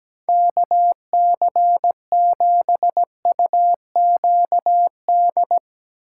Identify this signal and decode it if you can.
Morse code, heard as a keyed tone, about 18 wpm, 700 Hz. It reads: KC7UQD
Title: Speed Racing - USA 2x3 Call Signs 17wpm